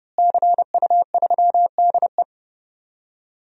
Morse code, heard as a keyed tone, about 32 wpm, 700 Hz. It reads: CU3DE
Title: DXCC-100 Call Signs - 5 Characters - Repeated 3x - 30wpm